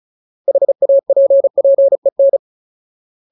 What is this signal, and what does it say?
Morse code, heard as a keyed tone, about 34 wpm, 550 Hz. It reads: HAPPEN
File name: Ultimate Word List / 35WPM Ultimate